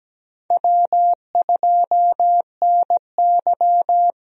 Morse code, heard as a keyed tone, about 17 wpm, 700 Hz. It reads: W2NY
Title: USA Call Signs - 4 Characters - Repeated 3x - 17wpm